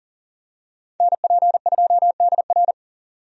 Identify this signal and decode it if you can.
Morse code, heard as a keyed tone, about 40 wpm, 700 Hz. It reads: NP2DR
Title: USA Call Signs - 5 Characters - 40wpm